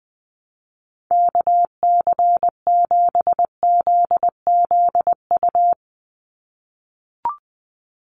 Morse code, heard as a keyed tone, about 20 wpm, 700 Hz. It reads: KC7ZZU
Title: USA Call Signs 20wpm